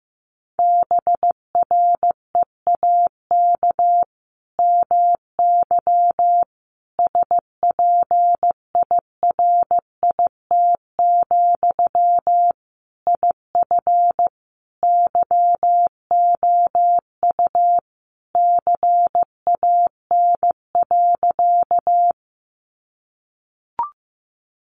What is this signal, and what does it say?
Morse code, heard as a keyed tone, about 15 wpm, 700 Hz. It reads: BREAK MY SPIRIT, IF YOU CAN.
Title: Sentences from Top 1400 Words 15wpm